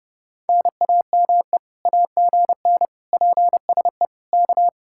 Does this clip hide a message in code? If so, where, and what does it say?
Yes: Morse code from 0.5 to 4.7 s, heard as a keyed tone, about 30 wpm, 700 Hz. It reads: NAME AGN PSE K